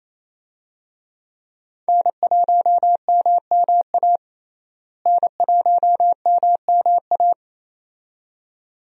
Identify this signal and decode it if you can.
Morse code, heard as a keyed tone, about 28 wpm, 700 Hz. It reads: N1MMA N1MMA
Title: USA Call Signs - 5 Characters - Repeated 2x - 28wpm